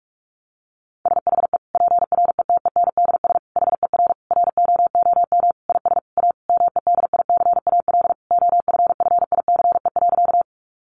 Morse code, heard as a keyed tone, about 44 wpm, 700 Hz. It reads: SHE PRETENDS HER ROOM IS A MEDICAL OFFICE.
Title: Additional Easy Phrases 45wpm